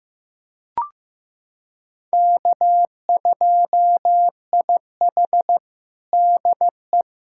Morse code, heard as a keyed tone, about 15 wpm, 700 Hz. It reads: K2IH DE
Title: USA Callsign DE Callsign - 4 Characters - 15wpm